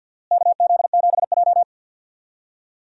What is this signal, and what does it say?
Morse code, heard as a keyed tone, about 49 wpm, 700 Hz. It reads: KZ7J
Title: USA Call Signs - 4 Characters - Repeated 3x - 50wpm